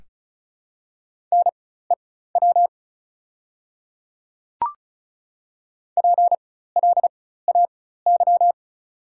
Morse code, heard as a keyed tone, about 36 wpm, spaced as for 16 wpm, 690 Hz. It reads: NEW PLAY
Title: 4x Character Spacing - Top 200 Words - 35wpm